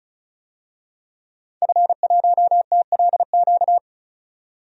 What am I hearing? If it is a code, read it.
Morse code, heard as a keyed tone, about 36 wpm, 700 Hz. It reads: F1TLQ
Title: DXCC-100 Call Signs - 5 Characters - 35wpm